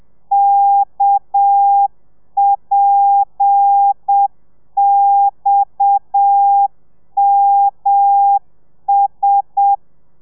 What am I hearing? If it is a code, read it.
Morse code, heard as a keyed tone, about 7 wpm, 800 Hz. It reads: KPXMS
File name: telegrafia / i2tfj / Lez7 30